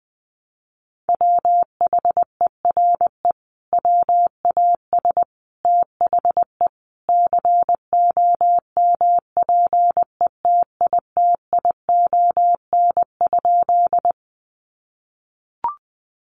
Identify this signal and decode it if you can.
Morse code, heard as a keyed tone, about 20 wpm, 700 Hz. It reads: WHERE WAS THE COMPETITION?
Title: Sentences from Top 1000 Words 20wpm